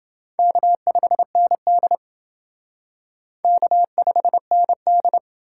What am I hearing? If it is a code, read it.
Morse code, heard as a keyed tone, about 28 wpm, 700 Hz. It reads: K5ND K5ND